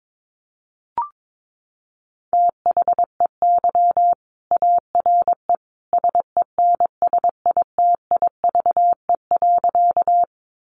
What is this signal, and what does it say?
Morse code, heard as a keyed tone, about 22 wpm, 700 Hz. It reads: THEY ARE SENSITIVE.
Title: Sentences from Top 1400 Words 22wpm